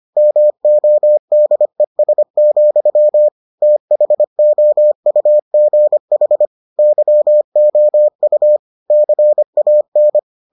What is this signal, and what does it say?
Morse code, heard as a keyed tone, about 25 wpm, 600 Hz. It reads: MODES, THOUGH YOU CAN